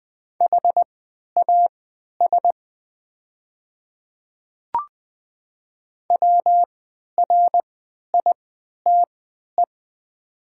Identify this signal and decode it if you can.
Morse code, heard as a keyed tone, about 20 wpm, spaced as for 11 wpm, 700 Hz. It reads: HAS WRITE
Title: 3x Character Spacing - Top 500 Words - 20wpm